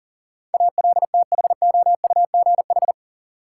Morse code, heard as a keyed tone, about 39 wpm, 700 Hz. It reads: ALTHOUGH
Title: Sentences from Top 600 Words - New Word Drill - 40wpm